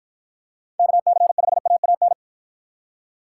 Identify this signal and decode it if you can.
Morse code, heard as a keyed tone, about 53 wpm, 700 Hz. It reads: KC5AAN